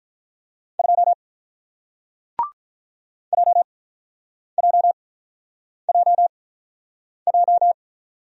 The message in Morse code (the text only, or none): 2 JJJJ